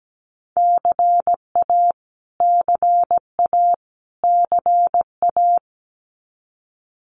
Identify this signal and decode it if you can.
Morse code, heard as a keyed tone, about 17 wpm, 700 Hz. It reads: CA CA CA
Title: US States Repeated 3x 17wpm